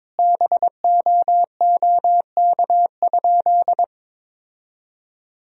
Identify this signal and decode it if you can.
Morse code, heard as a keyed tone, about 22 wpm, 700 Hz. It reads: BOOK?
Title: Sentences from Top 1000 Words - Review Part 1 - 22wpm